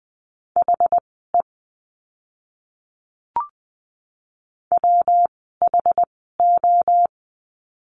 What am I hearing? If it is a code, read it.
Morse code, heard as a keyed tone, about 20 wpm, spaced as for 14 wpm, 700 Hz. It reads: HE WHO
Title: 244 Lesson 39f - Letter z - Words 20wpm